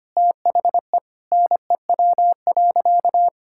Morse code, heard as a keyed tone, about 25 wpm, 700 Hz. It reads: THE NEW.
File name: practice / 25wpm / Sentences with 9-Word Phrases - 25wpm